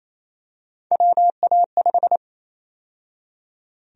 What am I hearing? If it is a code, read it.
Morse code, heard as a keyed tone, about 28 wpm, 700 Hz. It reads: WA5
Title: USA Call Sign Prefixes - 28wpm